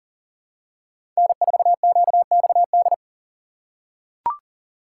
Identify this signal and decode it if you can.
Morse code, heard as a keyed tone, about 40 wpm, 700 Hz. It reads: N4QXD
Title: USA Call Signs - 5 Characters - 6x Character Spacing - 40wpm